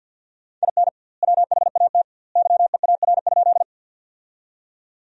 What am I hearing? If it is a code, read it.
Morse code, heard as a keyed tone, about 50 wpm, 700 Hz. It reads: IN WHAT YEAR?
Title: 2x Word-Spacing - Sentences from Top 200 Words - 50wpm